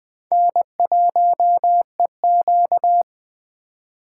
Morse code, heard as a keyed tone, about 21 wpm, 700 Hz. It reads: N1EQ
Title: USA Call Signs - 4 Characters - 3x Character Spacing - 20wpm